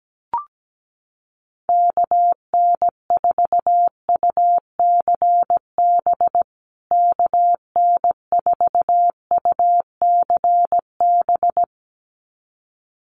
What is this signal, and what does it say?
Morse code, heard as a keyed tone, about 17 wpm, 700 Hz. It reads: KN4UCB KN4UCB
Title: USA Call Signs Repeated 2x 17wpm